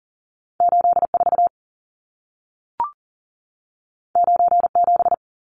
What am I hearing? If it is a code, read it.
Morse code, heard as a keyed tone, about 40 wpm, 700 Hz. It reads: 84 97
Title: Numbers - 2-Digit 40wpm